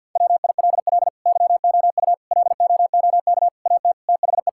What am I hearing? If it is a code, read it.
Morse code, heard as a keyed tone, about 46 wpm, 690 Hz. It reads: WILL YOU LOOK AT THE
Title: Sentences from Top 1000 Words - Review Part 5 - 50wpm